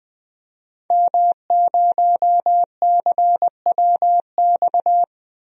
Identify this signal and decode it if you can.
Morse code, heard as a keyed tone, about 20 wpm, 700 Hz. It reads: M0CWX
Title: DXCC-100 Call Signs - 5 Characters - Repeated 3x - 20wpm